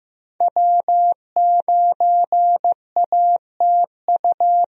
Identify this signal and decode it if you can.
Morse code, heard as a keyed tone, about 15 wpm, 700 Hz. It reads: W9ATU
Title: USA Call Signs - 5 Characters - Repeated 3x - 15wpm